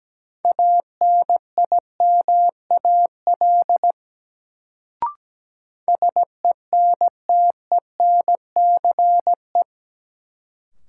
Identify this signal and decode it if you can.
Morse code, heard as a keyed tone, about 17 wpm, 700 Hz. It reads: ANIMAL SENTENCE
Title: Sentences from Top 1000 Words - Review Part 1 - New Word Drill - 17wpm